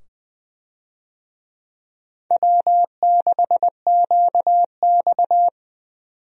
Morse code, heard as a keyed tone, about 20 wpm, 700 Hz. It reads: W6QX
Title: USA Call Signs - 4 Characters - 20wpm